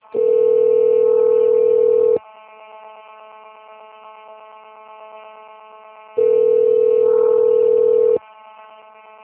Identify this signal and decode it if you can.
Ringback tone (North America), 440 + 480 Hz.
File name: rg-1600-unknown- - - .24923